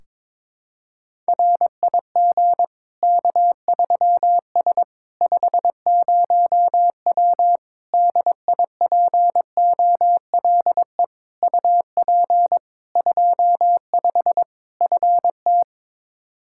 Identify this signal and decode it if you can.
Morse code, heard as a keyed tone, about 22 wpm, 700 Hz. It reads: RIG K3S 50W DIPOLE UP 25 FT
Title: Practice QSO - Exchange 2 - Info - 22wpm